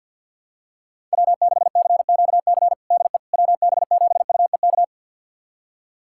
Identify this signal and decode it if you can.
Morse code, heard as a keyed tone, about 53 wpm, 700 Hz. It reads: W6CQC DE WB7UEK
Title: USA Callsign DE Callsign 50wpm